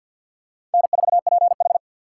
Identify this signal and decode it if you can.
Morse code, heard as a keyed tone, about 50 wpm, 700 Hz. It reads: N4PH
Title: USA Call Signs - 4 Characters - 3x Character Spacing - 50wpm